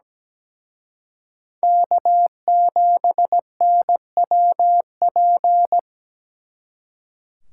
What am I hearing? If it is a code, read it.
Morse code, heard as a keyed tone, about 20 wpm, 700 Hz. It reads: K7NWP
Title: Speed Racing - USA 1x3 Call Signs 17wpm